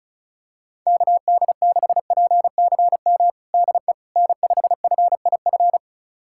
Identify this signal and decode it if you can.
Morse code, heard as a keyed tone, about 34 wpm, 700 Hz. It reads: KD6PCM DE N5FIF
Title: USA Callsign DE Callsign 35wpm